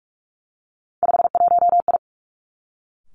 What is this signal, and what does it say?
Morse code, heard as a keyed tone, about 46 wpm, 700 Hz. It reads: 51I
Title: Three Challenging Letter-Number Combinations 45wpm